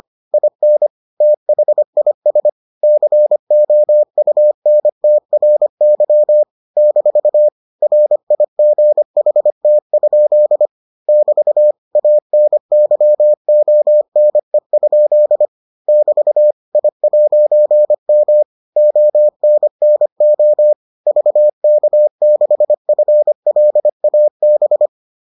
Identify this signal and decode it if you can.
Morse code, heard as a keyed tone, about 25 wpm, 600 Hz. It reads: IN THIS COUNTRY - RIGHT? <BT> ANYONE? <BT> I'M ONNO VK6FLAB